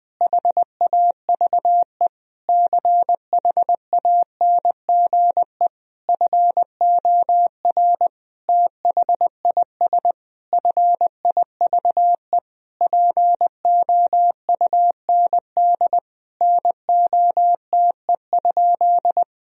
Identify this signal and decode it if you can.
Morse code, heard as a keyed tone, about 20 wpm, 700 Hz. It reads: HAVE CHANGE FOR THIS FIVE POUND NOTE?
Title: Sentences from Top 1000 Words - Review Part 2 - 20wpm